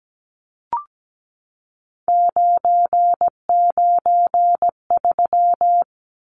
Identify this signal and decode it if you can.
Morse code, heard as a keyed tone, about 17 wpm, 700 Hz. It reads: 993
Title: Numbers - 3-Digit 17wpm